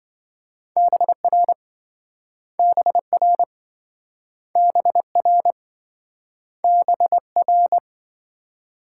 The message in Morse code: BR BR BR BR

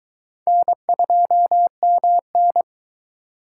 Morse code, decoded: N2MN

